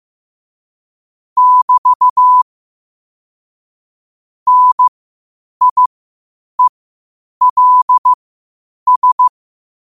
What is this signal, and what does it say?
Morse code, heard as a keyed tone, about 15 wpm, spaced as for 8 wpm, 1000 Hz. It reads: <BT> NIELS
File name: qotdfiles / 8wpm